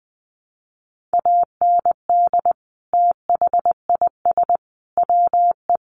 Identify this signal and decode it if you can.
Morse code, heard as a keyed tone, about 19 wpm, 700 Hz. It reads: AND THIS WE
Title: Sentences from Top 900 Words 20wpm